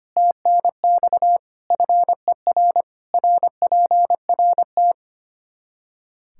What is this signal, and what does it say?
Morse code, heard as a keyed tone, about 25 wpm, 700 Hz. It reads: TNX FER RPRT
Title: K1USN SST QSO Practice 25wpm